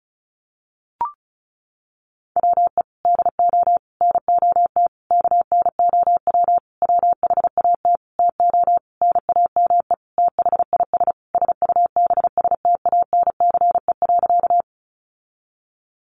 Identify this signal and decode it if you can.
Morse code, heard as a keyed tone, about 34 wpm, 700 Hz. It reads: WE DO NOT KNOW WHAT TO NAME THIS SUBSTANCE.
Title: Sentences from Top 1500 Words 35wpm